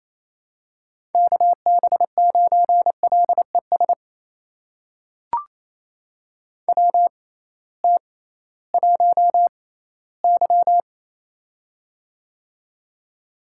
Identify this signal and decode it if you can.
Morse code, heard as a keyed tone, about 28 wpm, spaced as for 16 wpm, 700 Hz. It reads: KB9LES WT1Y
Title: USA Call Signs - 6x Character Spacing - 28wpm